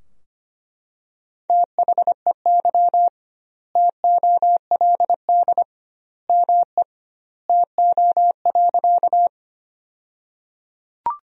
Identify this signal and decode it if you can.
Morse code, heard as a keyed tone, about 25 wpm, 700 Hz. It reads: THEY TOLD ME TO.
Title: 2x Word-Spacing - Sentences from Top 400 Words - 25wpm